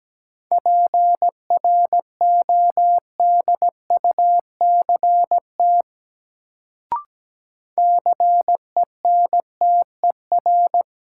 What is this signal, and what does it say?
Morse code, heard as a keyed tone, about 17 wpm, 700 Hz. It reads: PRODUCT CENTER
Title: Sentences from Top 1000 Words - Review Part 2 - New Word Drill - 17wpm